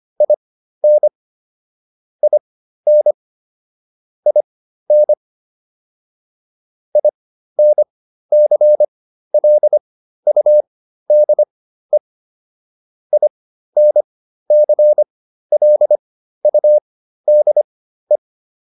Morse code, heard as a keyed tone, about 25 wpm, spaced as for 13 wpm, 600 Hz. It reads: IN IN IN INCLUDE INCLUDE
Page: intermediate-practice-files